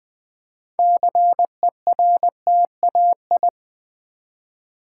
Morse code, heard as a keyed tone, about 20 wpm, 700 Hz. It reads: CERTAI
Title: Build A Word - 3 Letters At A Time - 9-Letter Words - 20wpm